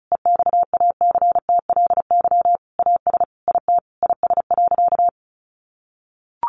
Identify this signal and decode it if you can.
Morse code, heard as a keyed tone, about 34 wpm, 700 Hz. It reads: EXACTLY AS IT IS.
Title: Sentences from Top 900 Words 35wpm